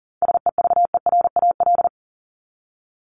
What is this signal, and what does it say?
Morse code, heard as a keyed tone, about 40 wpm, 700 Hz. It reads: SEVERAL
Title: Top 500 Words 40wpm